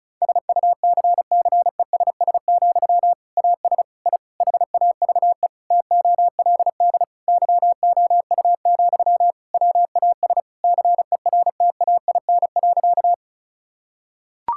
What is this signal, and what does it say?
Morse code, heard as a keyed tone, about 36 wpm, 700 Hz. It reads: SUCCESS, AS I HAVE TOLD YOU, WAS CERTAIN.
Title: Sentences from Top 1000 Words - Review Part 6 - 35wpm